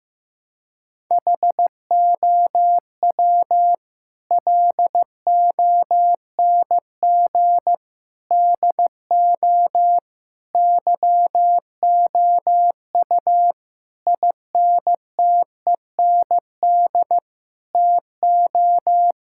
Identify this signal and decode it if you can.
Morse code, heard as a keyed tone, about 15 wpm, 700 Hz. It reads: HOW LONG DO YOU INTEND TO
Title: Sentences from Top 1600 Words 15wpm